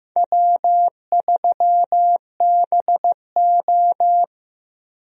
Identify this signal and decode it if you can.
Morse code, heard as a keyed tone, about 15 wpm, 700 Hz. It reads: W3BO
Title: USA Call Signs - 4 Characters - Repeated 2x - 15wpm